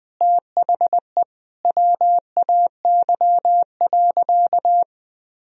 Morse code, heard as a keyed tone, about 21 wpm, 700 Hz. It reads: THE WAY.
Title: Sentences from Top 1000 Words - Review Part 1 - 20wpm